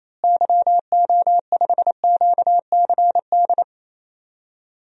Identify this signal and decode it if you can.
Morse code, heard as a keyed tone, about 28 wpm, 700 Hz. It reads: YO5QCD
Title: DXCC-100 Call Signs - 6 Characters - Repeated 2x - 28wpm